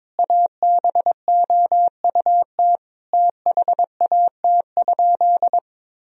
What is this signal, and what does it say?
Morse code, heard as a keyed tone, about 22 wpm, 700 Hz. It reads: ABOUT THAT?
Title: Sentences from Top 1000 Words - Review Part 4 - 22wpm